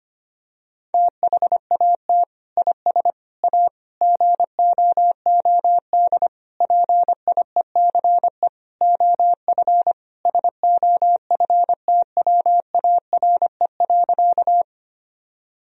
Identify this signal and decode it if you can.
Morse code, heard as a keyed tone, about 25 wpm, 700 Hz. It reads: THAT IS A GOOD PIECE OF SOFTWARE.
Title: Sentences from Top 1000 Words - Review Part 3 - 25wpm